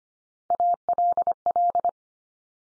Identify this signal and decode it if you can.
Morse code, heard as a keyed tone, about 25 wpm, 700 Hz. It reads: ALL
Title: Top 100 Words 25wpm x3